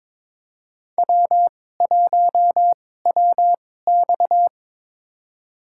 Morse code, heard as a keyed tone, about 22 wpm, spaced as for 16 wpm, 700 Hz. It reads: W1WX
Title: USA Call Signs - 4 Characters - 2x Character Spacing - 22wpm